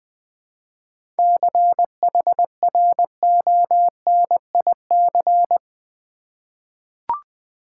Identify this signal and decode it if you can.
Morse code, heard as a keyed tone, about 20 wpm, 700 Hz. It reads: CHRONIC
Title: 7 Letter Words - First 3 Letters Given - 20wpm